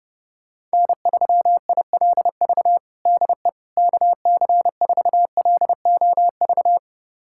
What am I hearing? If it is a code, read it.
Morse code, heard as a keyed tone, about 30 wpm, 700 Hz. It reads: N3ILV DE KC4LOV
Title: USA Callsign DE Callsign 30wpm